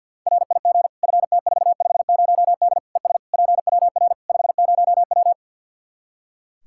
Morse code, heard as a keyed tone, about 46 wpm, 700 Hz. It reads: RIG FT450D ES PWR 50W
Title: K1USN SST QSO Practice 50wpm